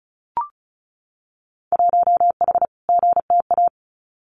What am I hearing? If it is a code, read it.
Morse code, heard as a keyed tone, about 36 wpm, 700 Hz. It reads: 1H GTA
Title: Winter Field Day Exchanges Encore 35wpm